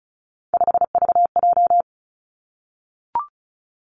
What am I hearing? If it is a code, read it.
Morse code, heard as a keyed tone, about 36 wpm, 700 Hz. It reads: 5VJ
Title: Three Challenging Letter-Number Combinations 35wpm